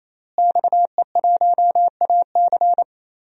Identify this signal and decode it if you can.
Morse code, heard as a keyed tone, about 27 wpm, 700 Hz. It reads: XE1AC
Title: DXCC-100 Call Signs - 5 Characters - Repeated 3x - 28wpm